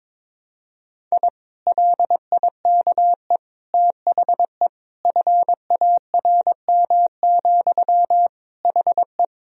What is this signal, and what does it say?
Morse code, heard as a keyed tone, about 22 wpm, 700 Hz. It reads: I LIKE THE FARM, HE
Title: Sentences from Top 1000 Words - Review Part 2 - 22wpm